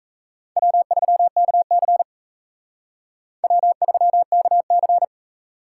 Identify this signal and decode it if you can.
Morse code, heard as a keyed tone, about 40 wpm, 700 Hz. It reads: W3KC W3KC